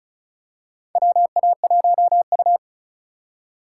Morse code, heard as a keyed tone, about 34 wpm, 700 Hz. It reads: WA1U
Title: USA Call Signs - 4 Characters - 6x Character Spacing - 35wpm